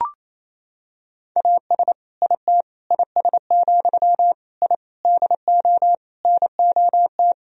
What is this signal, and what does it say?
Morse code, heard as a keyed tone, about 28 wpm, 700 Hz. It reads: AS IT IS, I DO NOT
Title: Sentences from Top 1000 Words - Review Part 2 - 28wpm